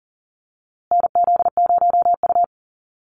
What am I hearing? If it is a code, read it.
Morse code, heard as a keyed tone, about 40 wpm, 700 Hz. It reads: NZ0U
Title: USA Call Signs Repeated 3x 40wpm